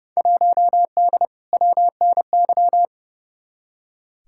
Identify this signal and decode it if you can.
Morse code, heard as a keyed tone, about 30 wpm, 700 Hz. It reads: 1D WNY
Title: Field Day Exchanges - 30wpm